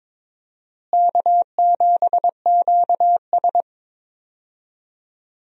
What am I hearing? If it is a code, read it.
Morse code, heard as a keyed tone, about 22 wpm, 700 Hz. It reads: K7QS
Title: USA Call Signs - 4 Characters - Repeated 3x - 22wpm